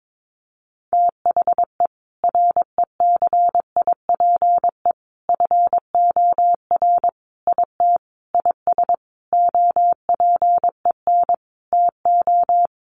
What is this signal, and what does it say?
Morse code, heard as a keyed tone, about 22 wpm, 700 Hz. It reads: THE RECIPE FOR IT IS OPEN TO
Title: Sentences from Top 1200 Words 22wpm